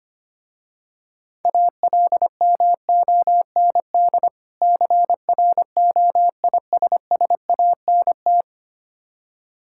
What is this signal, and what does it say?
Morse code, heard as a keyed tone, about 25 wpm, 700 Hz. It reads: ALMOND CROISSANT
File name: practice / 25wpm / Difficult Words with Hints - 25wpm